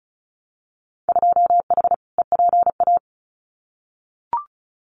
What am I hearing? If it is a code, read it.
Morse code, heard as a keyed tone, about 36 wpm, 700 Hz. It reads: 2H EPA
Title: Winter Field Day Exchanges 35wpm